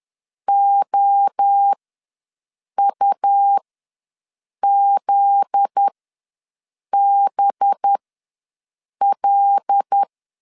Morse code, heard as a keyed tone, about 10 wpm, spaced as for 6 wpm, 800 Hz. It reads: OUZBL
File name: LEZIONI / 9 28 ws 10wpm 4124